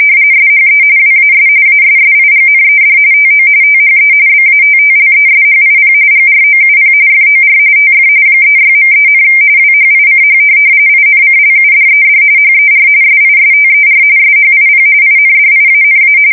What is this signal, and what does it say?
RTTY, about 45.45 baud. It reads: SHE WAS ASSIGNED TO THE BATTLESHIP GROUP FOR TF 36.3, WHICH INCLUDED HER SISTER SHIP INDIANA
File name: fsk170 / USS-Massachusetts